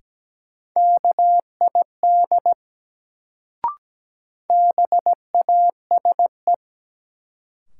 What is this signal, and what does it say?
Morse code, heard as a keyed tone, about 17 wpm, 700 Hz. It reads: KID BASE
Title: Sentences from Top 1600 Words - New Word Drill Alt - 17wpm